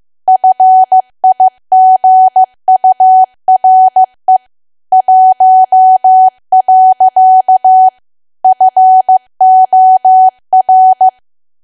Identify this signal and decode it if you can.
Morse code, heard as a keyed tone, about 15 wpm, 750 Hz. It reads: FIGURE 1. FOR
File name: cwa / W1AW-Level2 / 15WPM